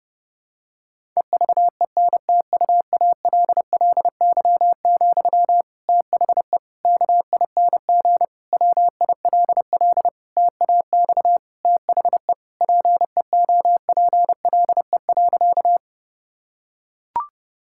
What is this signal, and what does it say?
Morse code, heard as a keyed tone, about 30 wpm, 700 Hz. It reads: EVENTUALLY, THE KING WILL TAX THE PEOPLE.
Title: Sentences from Top 1000 Words - Review Part 6 - 30wpm